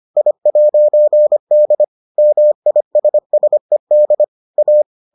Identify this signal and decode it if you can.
Morse code, heard as a keyed tone, about 25 wpm, 600 Hz. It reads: I'D MISSED A